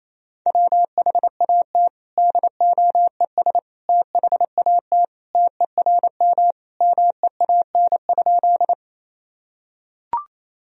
Morse code, heard as a keyed tone, about 28 wpm, 700 Hz. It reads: WHAT DOES THAT TERM MEAN?
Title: Sentences from Top 1000 Words - Review Part 4 - 28wpm